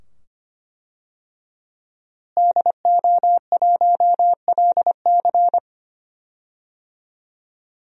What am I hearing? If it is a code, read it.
Morse code, heard as a keyed tone, about 25 wpm, 700 Hz. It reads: DO1LC
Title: DXCC-100 Call Signs - 5 Characters - Repeated 2x - 25wpm